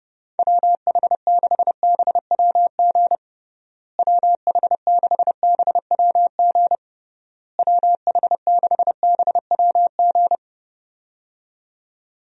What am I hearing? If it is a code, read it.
Morse code, heard as a keyed tone, about 32 wpm, 700 Hz. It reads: WH6BWG WH6BWG WH6BWG
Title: USA Call Signs - 6 Characters - Repeated 3x - 30wpm